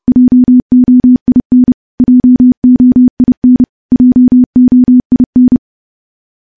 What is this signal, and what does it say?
Morse code, heard as a keyed tone, about 30 wpm, 250 Hz. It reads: JOIN JOIN JOIN
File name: morse / morse-training-30WPM